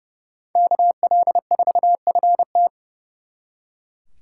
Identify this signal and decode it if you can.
Morse code, heard as a keyed tone, about 35 wpm, 700 Hz. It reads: KL4FT
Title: Speed Racing - USA Call Signs 30wpm